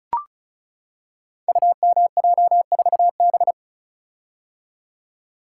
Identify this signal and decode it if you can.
Morse code, heard as a keyed tone, about 36 wpm, 700 Hz. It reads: UMJ4B
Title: Code Groups - 5 Letter:Numbers - 35wpm